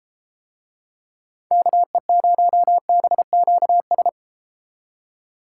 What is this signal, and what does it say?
Morse code, heard as a keyed tone, about 34 wpm, 700 Hz. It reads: KE0BQS